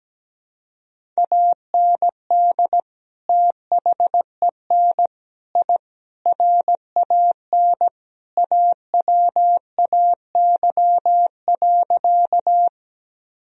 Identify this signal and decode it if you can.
Morse code, heard as a keyed tone, about 17 wpm, 700 Hz. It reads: AND THEN I RAN AWAY.
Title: Sentences from Top 1000 Words - Review Part 3 - 17wpm